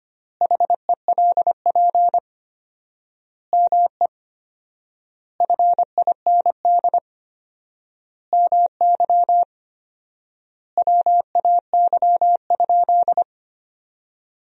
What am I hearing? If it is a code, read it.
Morse code, heard as a keyed tone, about 25 wpm, 700 Hz. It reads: HELP ME FIND MY WAY?
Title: 4x Word-Spacing - Sentences from Top 200 Words - 25wpm